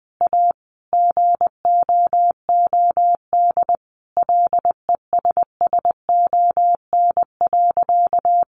Morse code, heard as a keyed tone, about 20 wpm, 700 Hz. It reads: A GOOD LESSON.
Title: Sentences from Top 1200 Words 20wpm